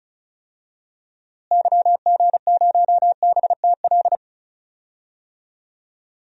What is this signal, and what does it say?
Morse code, heard as a keyed tone, about 36 wpm, 700 Hz. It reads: YG0BTL
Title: DXCC-100 Call Signs - 6 Characters - Repeated 2x - 35wpm